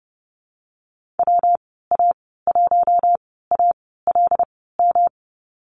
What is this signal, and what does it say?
Morse code, heard as a keyed tone, about 30 wpm, spaced as for 17 wpm, 700 Hz. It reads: WA1ALM
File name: Lesson 19g - Letter w - Call Signs 30wpm